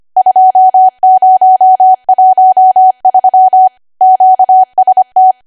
Morse code, heard as a keyed tone, about 25 wpm, 750 Hz. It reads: 2013 QST
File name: cwa / W1AW-Level3 / 25WPM